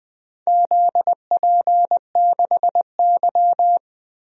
Morse code, heard as a keyed tone, about 20 wpm, 700 Hz. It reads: ZP6Y
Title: DXCC-100 Call Signs - 4 Characters - 20wpm